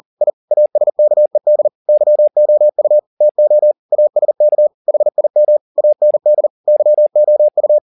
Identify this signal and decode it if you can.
Morse code, heard as a keyed tone, about 40 wpm, 600 Hz. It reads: I ASKED YOU TO ASK HIM AND YOU